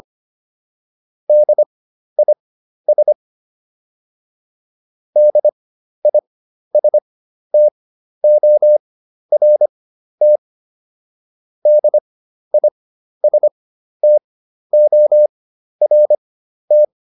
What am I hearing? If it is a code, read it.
Morse code, heard as a keyed tone, about 25 wpm, spaced as for 12 wpm, 600 Hz. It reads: DIS DISTORT DISTORT